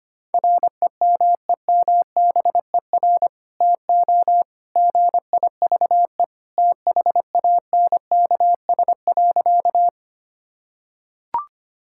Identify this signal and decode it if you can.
Morse code, heard as a keyed tone, about 25 wpm, 700 Hz. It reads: REMEMBER TO GIVE THANKS.
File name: practice / 25wpm / Sentences with 2-Word Phrases - 25wpm